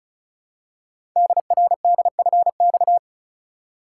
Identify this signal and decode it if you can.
Morse code, heard as a keyed tone, about 36 wpm, 700 Hz. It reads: DRDFX
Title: Code Groups - 5 Letters - 35wpm